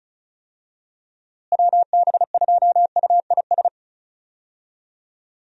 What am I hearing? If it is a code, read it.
Morse code, heard as a keyed tone, about 36 wpm, 700 Hz. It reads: WB2UIS
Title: USA Call Signs - 6 Characters - Repeated 2x - 35wpm